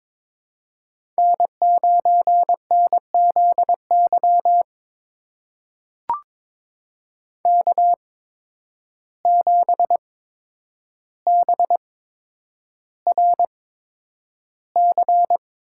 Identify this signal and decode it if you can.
Morse code, heard as a keyed tone, about 22 wpm, spaced as for 9 wpm, 700 Hz. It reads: N9NZY K7BRC
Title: USA Call Signs - 8x Character Spacing - 22wpm